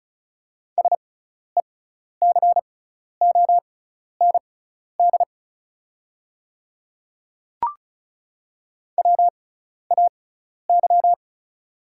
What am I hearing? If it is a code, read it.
Morse code, heard as a keyed tone, about 36 wpm, spaced as for 12 wpm, 690 Hz. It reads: SECOND WAY
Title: 6x Character Spacing - Top 300 Words - 35wpm